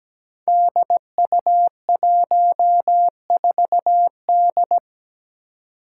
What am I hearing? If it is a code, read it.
Morse code, heard as a keyed tone, about 17 wpm, 700 Hz. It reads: DU14D
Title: Five Challenging Letter-Number Combinations 17wpm